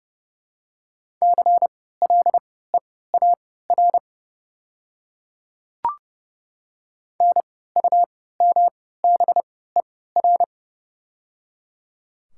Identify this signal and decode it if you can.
Morse code, heard as a keyed tone, about 29 wpm, spaced as for 17 wpm, 700 Hz. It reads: CLEAR NUMBER
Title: 3x Character Spacing - Top 500 Words - 30wpm